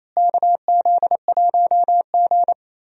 Morse code, heard as a keyed tone, about 28 wpm, 700 Hz. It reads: KZ1G
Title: USA Call Signs - 4 Characters - Repeated 3x - 28wpm